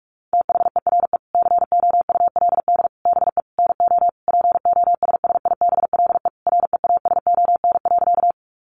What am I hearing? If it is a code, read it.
Morse code, heard as a keyed tone, about 44 wpm, 700 Hz. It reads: THERE COULD BE NO POSSIBLE REASON.
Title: Sentences from Top 700 Words 45wpm